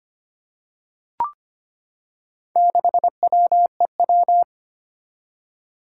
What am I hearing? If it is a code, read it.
Morse code, heard as a keyed tone, about 25 wpm, 700 Hz. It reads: 6WEW
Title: Four Challenging Letter-Number Combinations 25wpm